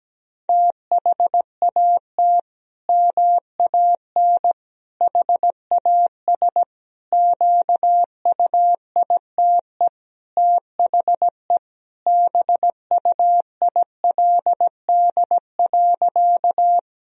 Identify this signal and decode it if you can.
Morse code, heard as a keyed tone, about 17 wpm, 700 Hz. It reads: THAT MAN HAS QUITE THE BUILD.
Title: Sentences from Top 1000 Words - Review Part 1 - 17wpm